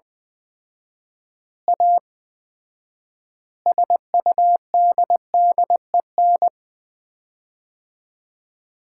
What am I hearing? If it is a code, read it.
Morse code, heard as a keyed tone, about 20 wpm, 700 Hz. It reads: A SUDDEN
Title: 4x Word-Spacing - Sets of 5 Words - 20wpm